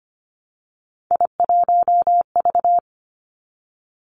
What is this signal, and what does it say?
Morse code, heard as a keyed tone, about 25 wpm, 700 Hz. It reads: I1V
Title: Three Challenging Letter-Number Combinations 25wpm